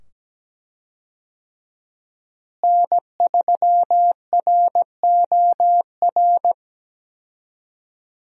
Morse code, heard as a keyed tone, about 17 wpm, 700 Hz. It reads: N3ROR
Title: USA Call Signs - 5 Characters - Repeated 2x - 17wpm